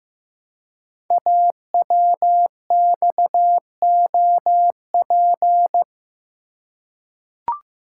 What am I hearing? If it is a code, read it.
Morse code, heard as a keyed tone, about 15 wpm, 700 Hz. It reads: AWXOP
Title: Code Groups - 5 Letters - 15wpm